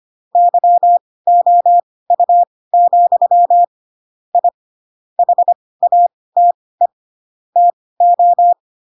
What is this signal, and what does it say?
Morse code, heard as a keyed tone, about 25 wpm, spaced as for 18 wpm, 700 Hz. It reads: YOU, I HATE TO
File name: Pinocchio / Pinocchio ch4 12 25 ik1hge